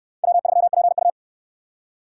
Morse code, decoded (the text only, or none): W3PU